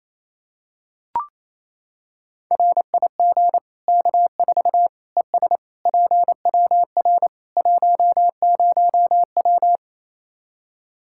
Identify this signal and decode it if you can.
Morse code, heard as a keyed tone, about 28 wpm, 700 Hz. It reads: RIG K4 ES PWR 10W
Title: Practice QSO - Exchange 2 - Info - 28wpm